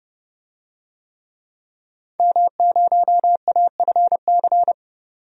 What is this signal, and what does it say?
Morse code, heard as a keyed tone, about 30 wpm, 700 Hz. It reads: M0AFC
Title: DXCC-100 Call Signs - 5 Characters - 30wpm